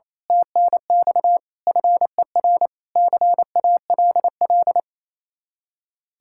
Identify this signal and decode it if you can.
Morse code, heard as a keyed tone, about 27 wpm, 700 Hz. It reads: TNX FER CALL
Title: Speed-Racing Common QSO Phrases Encore - 28wpm